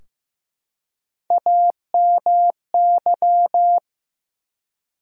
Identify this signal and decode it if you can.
Morse code, heard as a keyed tone, about 15 wpm, 700 Hz. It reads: AMY
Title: USA Common Names - Max 3 Letters - 15wpm